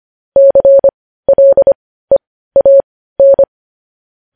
Morse code, heard as a keyed tone, about 25 wpm, spaced as for 15 wpm, 550 Hz. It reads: CLEAN
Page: News headlines in Morse code at 15 wpm